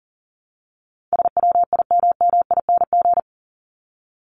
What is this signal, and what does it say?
Morse code, heard as a keyed tone, about 40 wpm, 700 Hz. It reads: SWIMMING
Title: Top 1501-1750 Words 40wpm